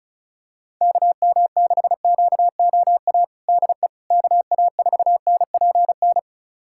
Morse code, heard as a keyed tone, about 35 wpm, 700 Hz. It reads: KM6QOA DE KA4NPN
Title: USA Callsign DE Callsign - 6 Characters - 35wpm